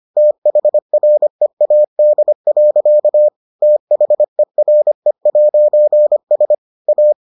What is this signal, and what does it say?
Morse code, heard as a keyed tone, about 25 wpm, 600 Hz. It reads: THREAD. THERE'S A